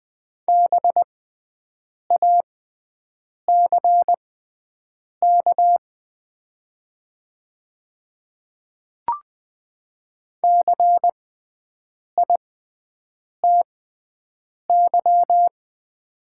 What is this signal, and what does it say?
Morse code, heard as a keyed tone, about 20 wpm, spaced as for 7 wpm, 690 Hz. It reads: BACK CITY
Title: 6x Character Spacing - Top 300 Words - 20wpm